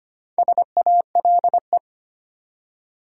Morse code, heard as a keyed tone, about 25 wpm, 700 Hz. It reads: SALE
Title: Four Letter Words - 25wpm